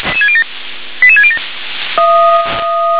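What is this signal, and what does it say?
Five-tone selective call (ZVEI), heard as quick runs of five tones, tones of about 70 ms. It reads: 88607
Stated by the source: ZVEI five-tone selective call 88726, received by SDR.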